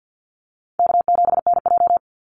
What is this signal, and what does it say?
Morse code, heard as a keyed tone, about 50 wpm, 700 Hz. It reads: K7NJ
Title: Easy Call Signs - 50wpm